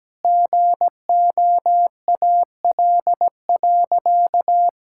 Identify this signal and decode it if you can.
Morse code, heard as a keyed tone, about 17 wpm, 700 Hz. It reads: GOAL.
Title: Sentences from Top 1000 Words - Review Part 6 - 17wpm